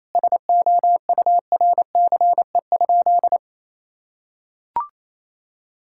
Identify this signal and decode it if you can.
Morse code, heard as a keyed tone, about 27 wpm, 700 Hz. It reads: SOURCE?
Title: Sentences from Top 1000 Words - Review Part 5 - 28wpm